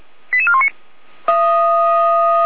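Five-tone selective call (ZVEI), heard as quick runs of five tones, tones of about 80 ms. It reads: 88419
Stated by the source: ZVEI five-tone selective call received by SDR.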